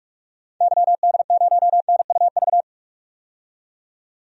Morse code, heard as a keyed tone, about 46 wpm, 700 Hz. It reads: YD0NUV
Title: DXCC-100 Call Signs - 6 Characters - 45wpm